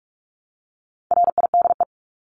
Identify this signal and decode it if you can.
Morse code, heard as a keyed tone, about 44 wpm, 700 Hz. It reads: RIDE
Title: Top 1001-1250 Words 45wpm